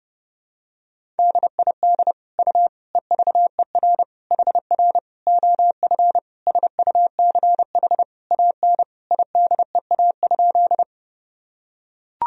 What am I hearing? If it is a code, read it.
Morse code, heard as a keyed tone, about 30 wpm, 700 Hz. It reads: DID U EVER HR OF SUCH AN IDEA?
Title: Sentences from Top 400 Words and Abbreviations 30wpm